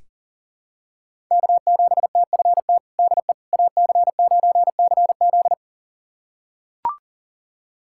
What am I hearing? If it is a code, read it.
Morse code, heard as a keyed tone, about 40 wpm, 700 Hz. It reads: K7TFT DE AC9CZ
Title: USA Callsign DE Callsign - 5 Characters - 40wpm